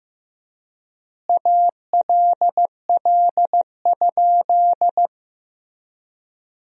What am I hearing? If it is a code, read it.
Morse code, heard as a keyed tone, about 16 wpm, 700 Hz. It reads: ALL?
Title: 3x Word-Spacing - Sentences from Top 100 Words - 15wpm